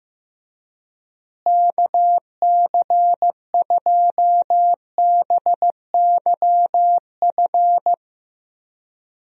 Morse code, heard as a keyed tone, about 16 wpm, 700 Hz. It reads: KC2BYF
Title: USA Call Signs - 6 Characters - Repeated 3x - 15wpm